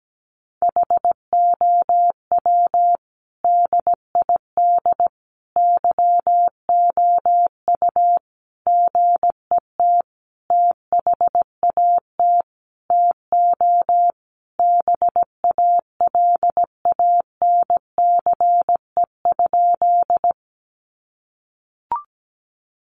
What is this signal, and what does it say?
Morse code, heard as a keyed tone, about 17 wpm, 700 Hz. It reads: HOW DID YOU GET THAT TO BALANCE?
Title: Sentences from Top 900 Words 17wpm